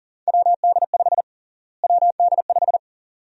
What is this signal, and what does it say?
Morse code, heard as a keyed tone, about 40 wpm, 700 Hz. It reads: WD5 WD5
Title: USA Call Sign Prefixes - Repeated 2x - 40wpm